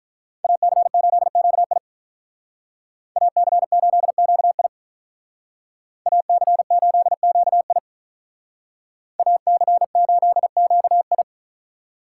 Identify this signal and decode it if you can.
Morse code, heard as a keyed tone, about 46 wpm, 700 Hz. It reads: AC8QI AC8QI AC8QI AC8QI